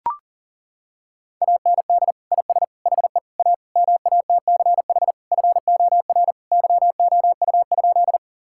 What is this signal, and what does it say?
Morse code, heard as a keyed tone, about 40 wpm, 700 Hz. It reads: AND IS HE A MATCH FOR YOU?
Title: Sentences from Top 1600 Words 40wpm